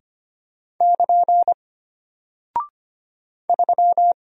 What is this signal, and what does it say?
Morse code, heard as a keyed tone, about 25 wpm, 700 Hz. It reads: <KN> 3
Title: Single Character (Letter:Number:Punctuation:Prosign) Alt 25wpm